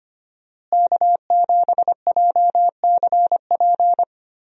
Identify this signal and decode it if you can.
Morse code, heard as a keyed tone, about 25 wpm, 700 Hz. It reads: K7JCP
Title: USA Call Signs - 5 Characters - Repeated 3x - 25wpm